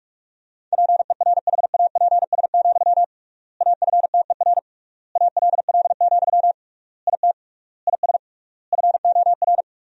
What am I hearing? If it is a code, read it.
Morse code, heard as a keyed tone, about 46 wpm, 700 Hz. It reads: PERHAPS, AFTER ALL, IT IS FOR
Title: 3x Word-Spacing - Sentences from Top 500 Words - 45wpm